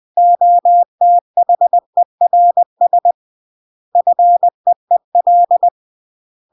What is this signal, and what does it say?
Morse code, heard as a keyed tone, about 20 wpm, 700 Hz. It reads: OTHERS FEEL
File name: CW / jodorowsky / 82 commandments 20-20